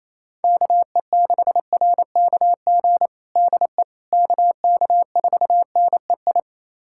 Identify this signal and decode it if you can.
Morse code, heard as a keyed tone, about 28 wpm, 700 Hz. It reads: KE6RKG DE KK4NEI
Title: USA Callsign DE Callsign 28wpm